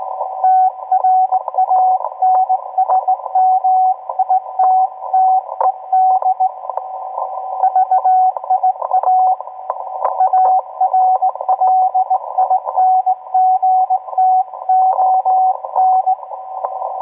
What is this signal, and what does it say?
Morse code, heard as a keyed tone, about 18 wpm, 750 Hz, with radio noise. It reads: TRANSMITTED VIA ARLINGTON